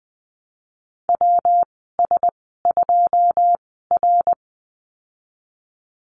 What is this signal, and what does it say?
Morse code, heard as a keyed tone, about 19 wpm, spaced as for 14 wpm, 700 Hz. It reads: WS2R
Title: 193 Lesson 31g - Letter v - Call Signs 20wpm